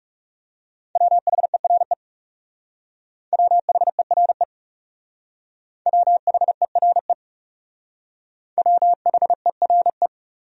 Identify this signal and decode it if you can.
Morse code, heard as a keyed tone, about 37 wpm, 700 Hz. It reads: WHERE WHERE WHERE WHERE
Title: Speed-Racing Top 500 Words 30wpm